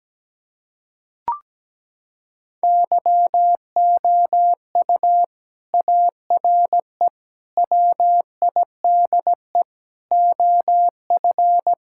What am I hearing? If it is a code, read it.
Morse code, heard as a keyed tone, about 17 wpm, 700 Hz. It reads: YOU ARE WIDE OF
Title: Sentences from Top 1000 Words - Review Part 6 - 17wpm